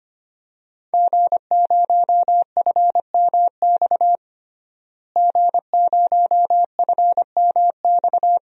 Morse code, heard as a keyed tone, about 25 wpm, 700 Hz. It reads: G0FMX G0FMX
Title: DXCC-100 Call Signs - 5 Characters - Repeated 2x - 25wpm